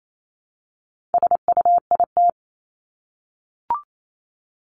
Morse code, heard as a keyed tone, about 28 wpm, 700 Hz. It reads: SUIT
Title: Top 1501-1750 Words 28wpm